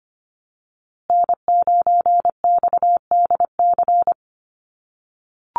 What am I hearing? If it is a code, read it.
Morse code, heard as a keyed tone, about 25 wpm, 700 Hz. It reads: N9XDC
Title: USA Call Signs Repeated 2x 25wpm